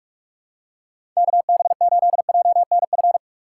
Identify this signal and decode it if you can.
Morse code, heard as a keyed tone, about 41 wpm, 700 Hz. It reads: KB8JNF
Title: USA Call Signs - 6 Characters - 2x Character Spacing - 45wpm